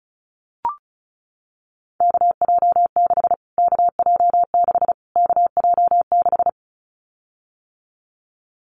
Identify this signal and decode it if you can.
Morse code, heard as a keyed tone, about 34 wpm, 700 Hz. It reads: KJ6 KJ6 KJ6
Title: Callsign Prefixes for Top 25 DXCC - Repeated 3x - 35wpm